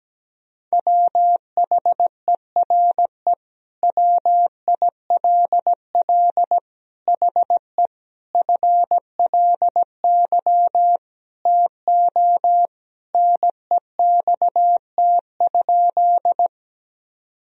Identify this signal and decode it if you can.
Morse code, heard as a keyed tone, about 17 wpm, 700 Hz. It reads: WHERE WILL HE FLY TO NEXT?
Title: Sentences from Top 1000 Words - Review Part 2 - 17wpm